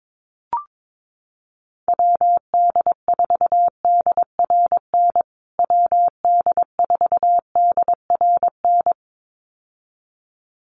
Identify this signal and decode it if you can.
Morse code, heard as a keyed tone, about 22 wpm, 700 Hz. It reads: WD4DRN WD4DRN
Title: USA Call Signs Repeated 2x 22wpm